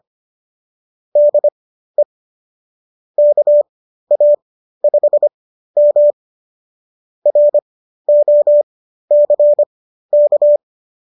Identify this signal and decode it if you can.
Morse code, heard as a keyed tone, about 25 wpm, spaced as for 13 wpm, 590 Hz. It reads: DE KA5M ROCK
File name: qso211 13